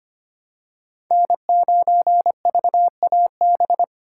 Morse code, heard as a keyed tone, about 25 wpm, 700 Hz. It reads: N9VAB
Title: USA Call Signs - 6x Character Spacing - 25wpm